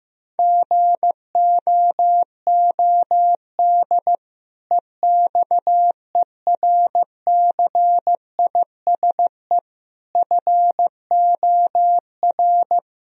Morse code, heard as a keyed tone, about 15 wpm, 700 Hz. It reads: GOOD EXERCISE FOR
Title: Sentences from Top 1000 Words - Review Part 4 - 15wpm